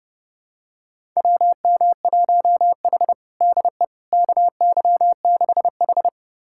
Morse code, heard as a keyed tone, about 30 wpm, 700 Hz. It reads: WM1H DE KY6H
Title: USA Callsign DE Callsign - 4 Characters - 30wpm